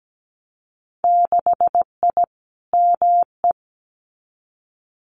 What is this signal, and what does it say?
Morse code, heard as a keyed tone, about 17 wpm, 700 Hz. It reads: 6I ME
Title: Winter Field Day Exchanges Encore 17wpm